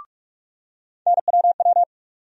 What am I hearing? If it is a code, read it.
Morse code, heard as a keyed tone, about 46 wpm, 700 Hz. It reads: NWW
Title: USA Call Sign Postfixes - 3 Letters 45wpm